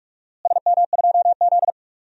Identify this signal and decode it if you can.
Morse code, heard as a keyed tone, about 44 wpm, 700 Hz. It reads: SM2Z
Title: DXCC-100 Call Signs - 4 Characters - Repeated 3x - 45wpm